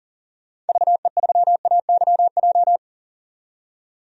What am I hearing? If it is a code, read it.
Morse code, heard as a keyed tone, about 38 wpm, 700 Hz. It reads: VE3AYJ
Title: DXCC-100 Call Signs - 6 Characters - 40wpm